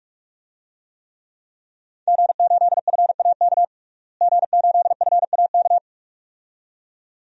Morse code, heard as a keyed tone, about 42 wpm, 700 Hz. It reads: G8FAK G8FAK
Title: DXCC-100 Call Signs - 5 Characters - Repeated 2x - 45wpm